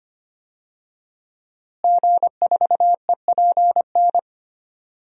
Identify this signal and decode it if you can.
Morse code, heard as a keyed tone, about 25 wpm, 700 Hz. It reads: G4EPN
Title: DXCC-100 Call Signs - 5 Characters - Repeated 2x - 25wpm